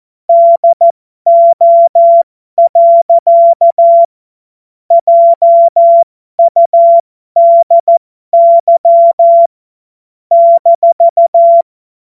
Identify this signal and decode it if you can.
Morse code, heard as a keyed tone, about 14 wpm, spaced as for 12 wpm, 680 Hz. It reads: DO. JUDY -
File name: Morsekurs / Witze